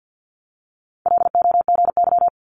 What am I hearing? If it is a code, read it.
Morse code, heard as a keyed tone, about 50 wpm, 700 Hz. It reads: LOGY
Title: Multi-letter Phonemes 50wpm